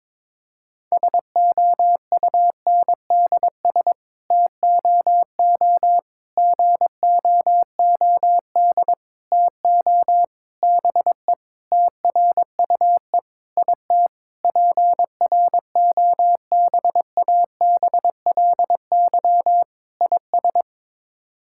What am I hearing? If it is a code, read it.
Morse code, heard as a keyed tone, about 22 wpm, 700 Hz. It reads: SOUNDS TOO GOOD TO BE TRUE IT PROBABLY IS
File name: Sets of 9 Words - 22wpm